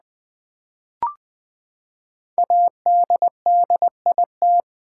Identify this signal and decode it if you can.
Morse code, heard as a keyed tone, about 20 wpm, 700 Hz. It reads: ADDIT
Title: Build A Word - 5 Letters At A Time - 12-Letter Words - 20wpm